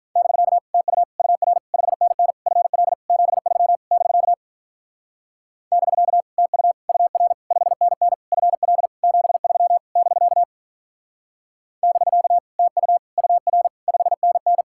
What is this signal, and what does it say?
Morse code, heard as a keyed tone, about 42 wpm, 700 Hz. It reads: <BK> TU UR 5NN FL 73 <BK> <BK> TU UR 5NN FL 73 <BK> <BK> TU UR 5NN
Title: Speed-Racing Common QSO Phrases - 35wpm